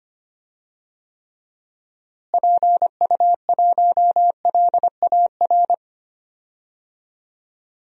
Morse code, heard as a keyed tone, about 25 wpm, 700 Hz. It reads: PU1LAR
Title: DXCC-100 Call Signs - 6 Characters - Repeated 3x - 25wpm